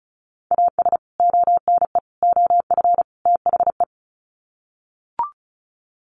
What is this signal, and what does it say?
Morse code, heard as a keyed tone, about 34 wpm, 700 Hz. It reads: AS ONE OF THE
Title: Sets of 4 Words 35wpm